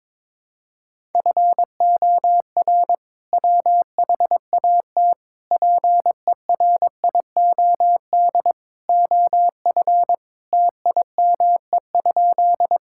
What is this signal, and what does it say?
Morse code, heard as a keyed tone, about 22 wpm, 700 Hz. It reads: FOR WHAT PERIOD OF TIME?
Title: Sentences from Top 1000 Words - Review Part 5 - 22wpm